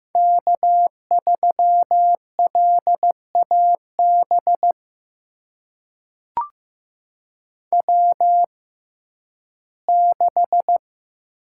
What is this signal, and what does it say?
Morse code, heard as a keyed tone, about 15 wpm, spaced as for 11 wpm, 700 Hz. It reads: K3LAB W6
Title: USA Call Signs - 5 Characters - 6x Character Spacing - 15wpm